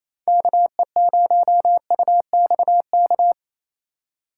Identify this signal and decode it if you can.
Morse code, heard as a keyed tone, about 28 wpm, 700 Hz. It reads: KE0UXK
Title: USA Call Signs - 6 Characters - 2x Character Spacing - 28wpm